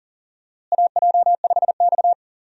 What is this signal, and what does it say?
Morse code, heard as a keyed tone, about 40 wpm, 700 Hz. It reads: AJ5X
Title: USA Call Signs - 4 Characters - Repeated 2x - 40wpm